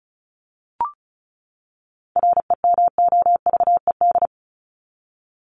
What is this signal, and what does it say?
Morse code, heard as a keyed tone, about 34 wpm, 700 Hz. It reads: REMOVED
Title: ED Suffix - Top 100 Words with Base Words 35wpm